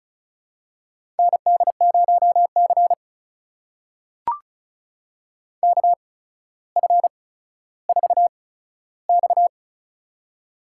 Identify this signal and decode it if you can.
Morse code, heard as a keyed tone, about 36 wpm, spaced as for 15 wpm, 700 Hz. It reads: ND0C KF4X
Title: USA Call Signs - 4 Characters - 8x Character Spacing - 35wpm